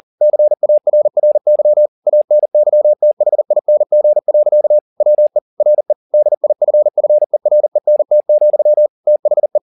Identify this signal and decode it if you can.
Morse code, heard as a keyed tone, about 40 wpm, 600 Hz. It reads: CARRY ANYTHING. WE RE DIFFERENT, THE